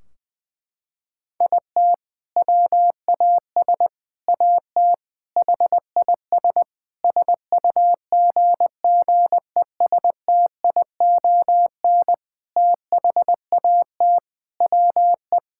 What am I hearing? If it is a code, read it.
Morse code, heard as a keyed tone, about 20 wpm, 700 Hz. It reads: IT WAS AT HIS SUGGESTION THAT WE
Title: Sentences from Top 1600 Words 20wpm